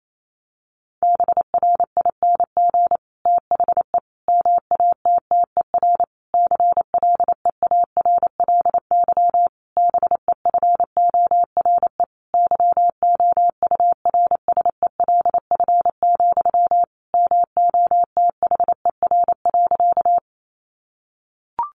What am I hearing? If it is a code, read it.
Morse code, heard as a keyed tone, about 28 wpm, 700 Hz. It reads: BRING THE MATTER CLEARLY BEFORE YOURSELF, MOTHER.
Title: Sentences from Top 1300 Words 28wpm